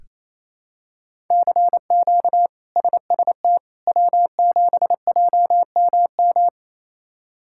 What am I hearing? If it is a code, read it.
Morse code, heard as a keyed tone, about 28 wpm, 700 Hz. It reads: CQ SST W7JMM
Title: K1USN SST QSO Practice - 28wpm